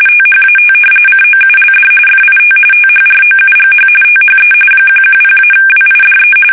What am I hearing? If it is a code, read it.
RTTY, about 45.45 baud. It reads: THAT COULD FIT THROUGH THE 110-WIDE
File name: fsk850 / battleshipnj